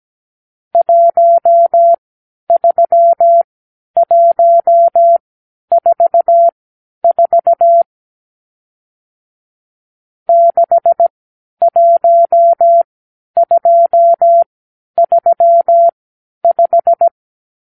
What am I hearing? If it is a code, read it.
Morse code, heard as a keyed tone, about 17 wpm, spaced as for 10 wpm, 680 Hz. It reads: 13144 61235
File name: k7qo / Lesson 50 (1-6)